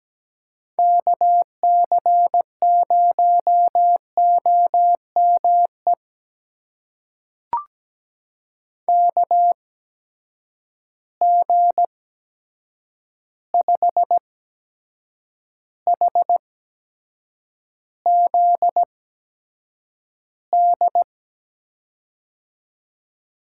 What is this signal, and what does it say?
Morse code, heard as a keyed tone, about 17 wpm, spaced as for 7 wpm, 700 Hz. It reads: KC0OME KG5HZD
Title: USA Call Signs - 6 Characters - 8x Character Spacing - 17wpm